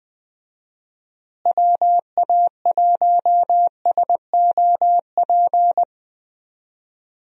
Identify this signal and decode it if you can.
Morse code, heard as a keyed tone, about 20 wpm, 700 Hz. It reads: WA1SOP
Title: USA Call Signs - 6 Characters - Repeated 2x - 20wpm